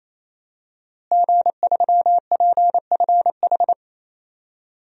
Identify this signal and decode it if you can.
Morse code, heard as a keyed tone, about 27 wpm, 700 Hz. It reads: G3PFH
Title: DXCC-100 Call Signs - 5 Characters - Repeated 3x - 28wpm